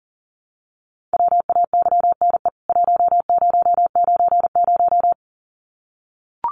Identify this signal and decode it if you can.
Morse code, heard as a keyed tone, about 40 wpm, 700 Hz. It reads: WAYNE 1090
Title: CWT Exchange 40wpm